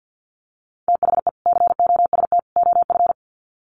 Morse code, heard as a keyed tone, about 50 wpm, 700 Hz. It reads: THE COST OF
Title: Sets of 3 Words 50wpm